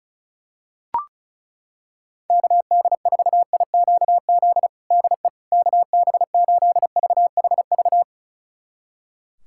Morse code, heard as a keyed tone, about 36 wpm, 700 Hz. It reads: KD4IQZ DE KB8VHV
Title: USA Callsign DE Callsign - 6 Characters - 35wpm